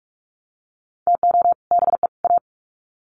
Morse code, heard as a keyed tone, about 44 wpm, 700 Hz. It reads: TO BE A
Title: Sets of 3 Words 45wpm